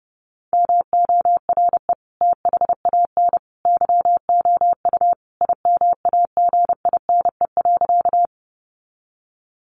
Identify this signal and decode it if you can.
Morse code, heard as a keyed tone, about 30 wpm, 700 Hz. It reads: MORE THAN YOU IMAGINE.
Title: Sentences from Top 1400 Words 30wpm